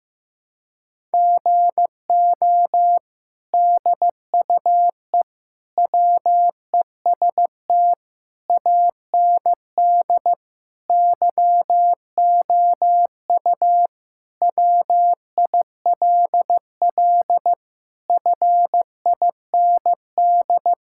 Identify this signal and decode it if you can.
Morse code, heard as a keyed tone, about 15 wpm, 700 Hz. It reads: GO DUE WEST AND YOU WILL FIND
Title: Sentences from Top 1000 Words - Review Part 6 - 15wpm